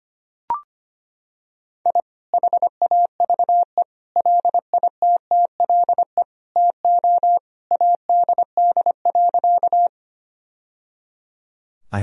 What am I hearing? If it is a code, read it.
Morse code, heard as a keyed tone, about 25 wpm, 700 Hz. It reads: I HAVE LITTLE TO ADD.
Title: Sentences from Top 1000 Words - Review Part 1 - 25wpm